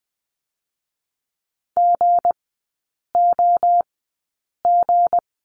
Morse code, heard as a keyed tone, about 20 wpm, spaced as for 8 wpm, 690 Hz. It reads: GOG
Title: 159 Lesson 26e - Letter g - Three Characters 20wpm